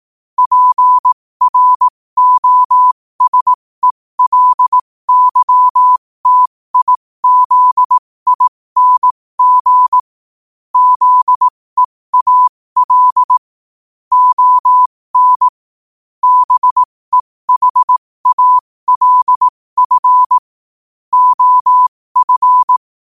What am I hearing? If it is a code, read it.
Morse code, heard as a keyed tone, about 18 wpm, spaced as for 15 wpm, 1000 Hz. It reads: PROSELYTIZING ZEAL ON BEHALF OF
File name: qotdfiles / 15wpm